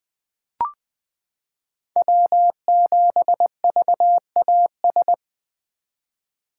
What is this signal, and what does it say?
Morse code, heard as a keyed tone, about 20 wpm, 700 Hz. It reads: W7VAS
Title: K1USN SST QSO Practice - 20wpm